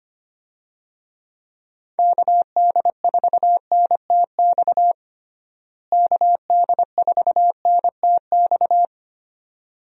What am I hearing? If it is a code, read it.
Morse code, heard as a keyed tone, about 25 wpm, 700 Hz. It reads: KD4NTX KD4NTX
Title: USA Call Signs - 6 Characters - Repeated 2x - 25wpm